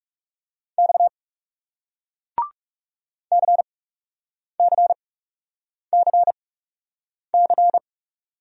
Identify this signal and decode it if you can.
Morse code, heard as a keyed tone, about 40 wpm, spaced as for 8 wpm, 700 Hz. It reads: X CCCC